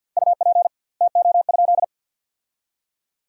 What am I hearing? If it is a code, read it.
Morse code, heard as a keyed tone, about 50 wpm, 700 Hz. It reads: UP TO?
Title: 2x Word-Spacing - Sentences from Top 100 Words - 50wpm